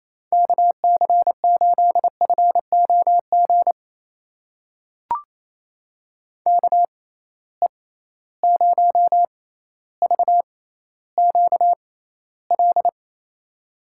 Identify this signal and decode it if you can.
Morse code, heard as a keyed tone, about 28 wpm, spaced as for 14 wpm, 700 Hz. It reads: KC8FOG KE0VQL
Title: USA Call Signs - 6 Characters - 6x Character Spacing - 28wpm